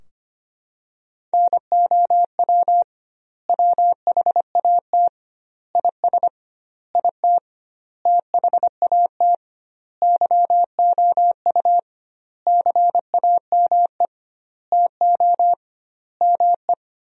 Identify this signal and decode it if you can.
Morse code, heard as a keyed tone, about 25 wpm, 700 Hz. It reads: NOW WHAT IS IT THAT YOU CAME TO ME
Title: 2x Word-Spacing - Sentences from Top 200 Words - 25wpm